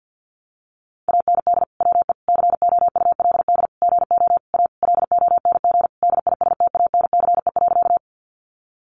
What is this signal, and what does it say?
Morse code, heard as a keyed tone, about 50 wpm, 700 Hz. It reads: AND WE COULD GO A LONG DISTANCE.
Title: Sentences from Top 1200 Words 50wpm